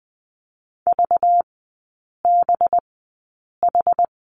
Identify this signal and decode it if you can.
Morse code, heard as a keyed tone, about 20 wpm, spaced as for 8 wpm, 700 Hz. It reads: VBH
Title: 191 Lesson 31e - Letter v - Three Characters 20wpm